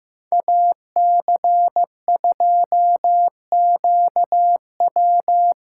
Morse code, heard as a keyed tone, about 15 wpm, 700 Hz. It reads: AC2QW
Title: USA Callsign DE Callsign - 5 Characters - 15wpm